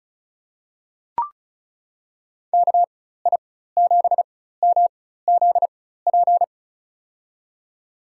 Morse code, heard as a keyed tone, about 36 wpm, spaced as for 16 wpm, 700 Hz. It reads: KI7MZP
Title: USA Call Signs - 4x Character Spacing - 35wpm